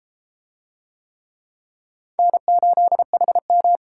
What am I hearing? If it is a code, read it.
Morse code, heard as a keyed tone, about 30 wpm, 700 Hz. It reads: N8HM